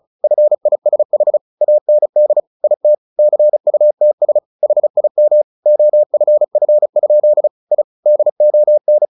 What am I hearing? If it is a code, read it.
Morse code, heard as a keyed tone, about 34 wpm, 600 Hz. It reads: FISH AND IT CUTS HIM OFF? I DON
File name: the old man and the sea / the old man and the sea 1 35wpm